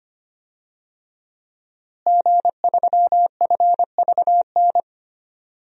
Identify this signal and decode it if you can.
Morse code, heard as a keyed tone, about 25 wpm, 700 Hz. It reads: G3FVN
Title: DXCC-100 Call Signs - 5 Characters - Repeated 3x - 25wpm